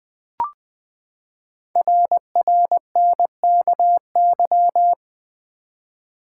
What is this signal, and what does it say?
Morse code, heard as a keyed tone, about 20 wpm, 700 Hz. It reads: RRNKY
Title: Code Groups - 5 Letters - 20wpm